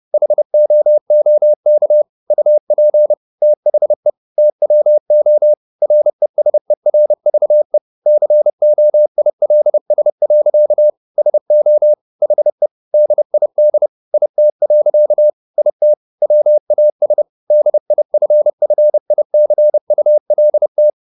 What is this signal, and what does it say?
Morse code, heard as a keyed tone, about 30 wpm, 600 Hz. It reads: HOOK UP THE TWO RESERVE COILS. SO HE DID IT. IT WAS DIFFICULT
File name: the old man and the sea / the old man and the sea 1 30wpm